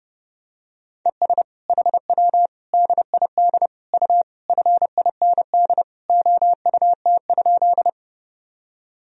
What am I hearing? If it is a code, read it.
Morse code, heard as a keyed tone, about 30 wpm, 700 Hz. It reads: ES HW DID U FIND OUT?
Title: Sentences from Top 200 Words and Abbreviations 30wpm